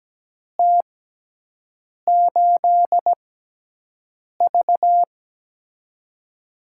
Morse code, heard as a keyed tone, about 17 wpm, spaced as for 6 wpm, 700 Hz. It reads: T8V
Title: USA Call Signs - 4 Characters - 6x Character Spacing - 17wpm